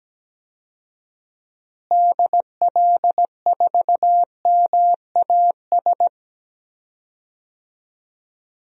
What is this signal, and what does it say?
Morse code, heard as a keyed tone, about 17 wpm, 700 Hz. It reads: DL4MAS
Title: DXCC-100 Call Signs - 6 Characters - Repeated 3x - 17wpm